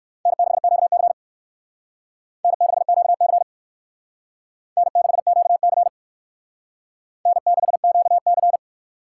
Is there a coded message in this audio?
Yes: Morse code from 0.3 to 8.6 s, heard as a keyed tone, about 56 wpm, 700 Hz. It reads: N6QC N6QC N6QC N6QC